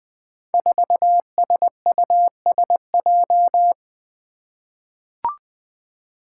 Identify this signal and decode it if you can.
Morse code, heard as a keyed tone, about 20 wpm, 700 Hz. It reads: 4SUSJ
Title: Five Challenging Letter-Number Combinations 20wpm